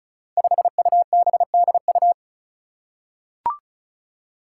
Morse code, heard as a keyed tone, about 36 wpm, 700 Hz. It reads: 5UBDU
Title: Five Challenging Letter-Number Combinations 35wpm